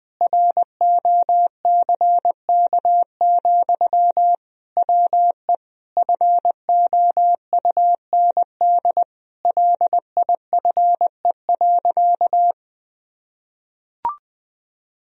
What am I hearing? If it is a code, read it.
Morse code, heard as a keyed tone, about 20 wpm, 700 Hz. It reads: ROCK, WE FOUND LIFE.
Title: Sentences from Top 1000 Words - Review Part 2 - 20wpm